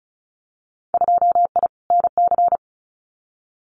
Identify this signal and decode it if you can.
Morse code, heard as a keyed tone, about 34 wpm, 700 Hz. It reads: 2I NC
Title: Winter Field Day Exchanges 35wpm